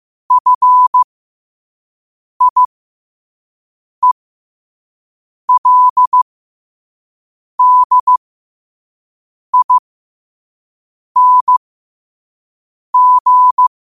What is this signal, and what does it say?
Morse code, heard as a keyed tone, about 15 wpm, spaced as for 5 wpm, 1000 Hz. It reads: FIELDING
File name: qotdfiles / 5wpm